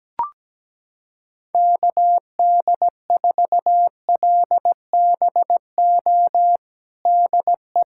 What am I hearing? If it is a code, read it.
Morse code, heard as a keyed tone, about 17 wpm, 700 Hz. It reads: KD4LBO DE
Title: USA Callsign DE Callsign - 6 Characters - 17wpm